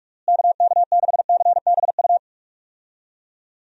Morse code, heard as a keyed tone, about 46 wpm, 700 Hz. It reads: KK6CBU
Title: USA Call Signs - 6 Characters - Repeated 3x - 45wpm